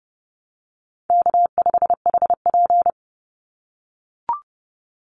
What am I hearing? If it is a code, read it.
Morse code, heard as a keyed tone, about 30 wpm, 700 Hz. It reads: K5HP
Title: Easy Call Signs - 30wpm